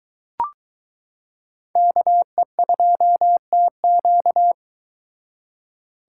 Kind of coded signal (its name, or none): Morse code